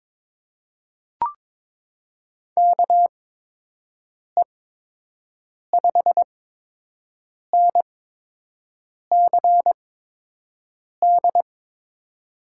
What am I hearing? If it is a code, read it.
Morse code, heard as a keyed tone, about 22 wpm, spaced as for 6 wpm, 700 Hz. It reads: KE5NCD
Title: USA Call Signs - 6 Characters - 8x Character Spacing - 22wpm